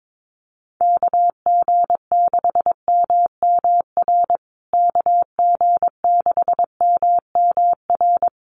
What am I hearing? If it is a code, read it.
Morse code, heard as a keyed tone, about 22 wpm, 700 Hz. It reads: KG6MMR KG6MMR
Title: USA Call Signs Repeated 3x 22wpm